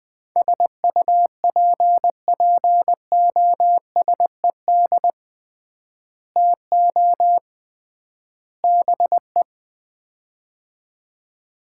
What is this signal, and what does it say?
Morse code, heard as a keyed tone, about 20 wpm, 700 Hz. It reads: SUPPOSED TO BE
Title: 3x Word-Spacing - Sets of 3 Words - 20wpm